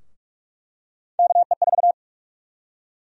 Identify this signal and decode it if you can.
Morse code, heard as a keyed tone, about 46 wpm, 700 Hz. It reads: KE4
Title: USA Call Sign Prefixes - 45wpm